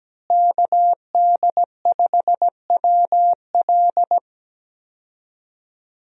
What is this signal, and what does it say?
Morse code, heard as a keyed tone, about 17 wpm, 700 Hz. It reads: KD5WL
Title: USA Call Signs - 5 Characters - Repeated 2x - 17wpm